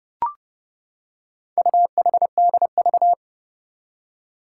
Morse code, heard as a keyed tone, about 28 wpm, 700 Hz. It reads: UHDV
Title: Four Challenging Letter-Number Combinations 30wpm